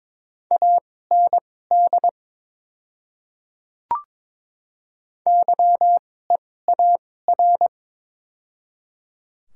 Morse code, heard as a keyed tone, about 21 wpm, spaced as for 16 wpm, 700 Hz. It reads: AND YEAR
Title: 2x Character Spacing - Top 200 Words - 22wpm